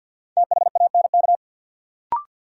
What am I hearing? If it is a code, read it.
Morse code, heard as a keyed tone, about 52 wpm, 700 Hz. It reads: THANK
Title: Sets of 2 Words - New Word Drill - 50wpm